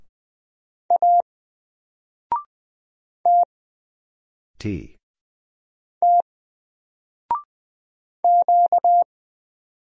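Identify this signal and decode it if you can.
Morse code, heard as a keyed tone, about 20 wpm, 700 Hz. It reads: A T T Q
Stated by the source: Single letter and number Morse drill at 20 wpm, from N4LCD.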